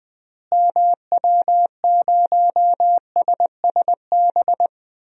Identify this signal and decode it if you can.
Morse code, heard as a keyed tone, about 20 wpm, 700 Hz. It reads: MW0SSB
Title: DXCC-100 Call Signs - 6 Characters - Repeated 3x - 20wpm